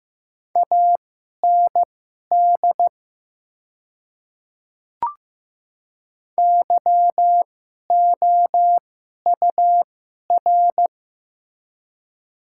Morse code, heard as a keyed tone, about 15 wpm, spaced as for 11 wpm, 690 Hz. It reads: AND YOUR
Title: 2x Character Spacing - Top 300 Words - 15wpm